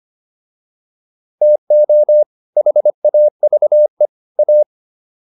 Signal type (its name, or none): Morse code